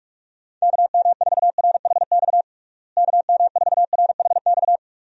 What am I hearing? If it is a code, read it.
Morse code, heard as a keyed tone, about 46 wpm, 700 Hz. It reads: KM4RHX KM4RHX
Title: USA Call Signs - 6 Characters - Repeated 3x - 45wpm